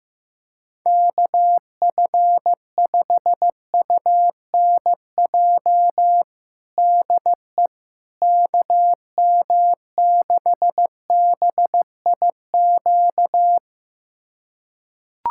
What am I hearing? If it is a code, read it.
Morse code, heard as a keyed tone, about 15 wpm, 700 Hz. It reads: KF5UNJ DE KM6BIQ
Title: USA Callsign DE Callsign 15wpm